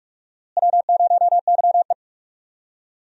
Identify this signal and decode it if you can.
Morse code, heard as a keyed tone, about 42 wpm, 700 Hz. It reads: W0YE
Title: USA Call Signs - 4 Characters - Repeated 3x - 45wpm